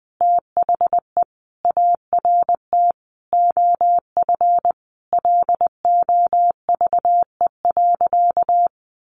Morse code, heard as a keyed tone, about 20 wpm, 700 Hz. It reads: THE ART OF LOVE.
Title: Sentences from Top 1500 Words 20wpm